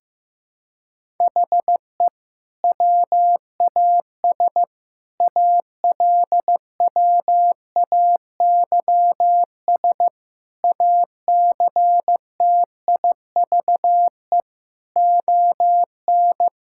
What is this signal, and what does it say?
Morse code, heard as a keyed tone, about 15 wpm, 700 Hz. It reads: HE WAS ALWAYS ACTIVE ON
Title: Sentences from Top 1000 Words - Review Part 6 - 15wpm